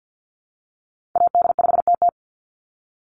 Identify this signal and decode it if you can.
Morse code, heard as a keyed tone, about 50 wpm, 700 Hz. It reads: AD5TT
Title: Call Signs - 50wpm